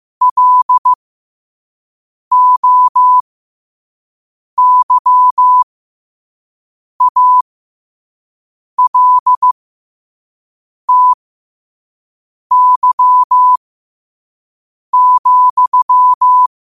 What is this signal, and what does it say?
Morse code, heard as a keyed tone, about 15 wpm, spaced as for 5 wpm, 1000 Hz. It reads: LOYALTY,
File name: qotdfiles / 5wpm